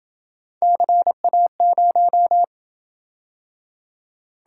Morse code, heard as a keyed tone, about 27 wpm, 700 Hz. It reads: CA0